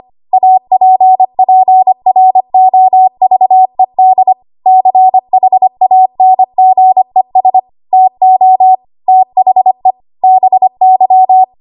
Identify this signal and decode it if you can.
Morse code, heard as a keyed tone, about 25 wpm, 750 Hz. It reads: APPROVED CHANGES TO THE BY
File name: Archive / 25 WPM / 25WPM